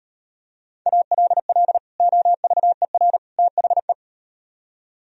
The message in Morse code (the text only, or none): ALL OVER THE